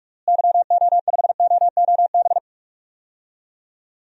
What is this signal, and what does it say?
Morse code, heard as a keyed tone, about 41 wpm, 700 Hz. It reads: YO5OOB
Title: DXCC-100 Call Signs - 6 Characters - Repeated 3x - 45wpm